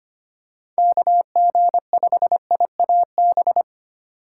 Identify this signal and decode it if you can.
Morse code, heard as a keyed tone, about 25 wpm, 700 Hz. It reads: KG5IAB
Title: USA Call Signs - 4x Character Spacing - 25wpm